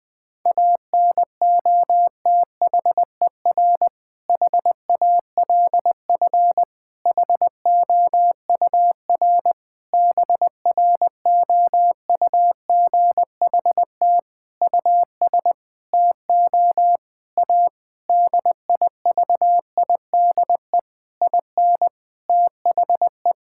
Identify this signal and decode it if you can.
Morse code, heard as a keyed tone, about 21 wpm, 700 Hz. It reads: ANOTHER HALF HOUR BROUGHT US TO A DIVIDE IN THE
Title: Sentences from Top 1600 Words 20wpm